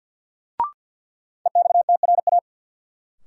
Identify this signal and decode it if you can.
Morse code, heard as a keyed tone, about 53 wpm, 700 Hz. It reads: EXTRA
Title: Sentences from Top 1000 Words - New Word Drill - 50wpm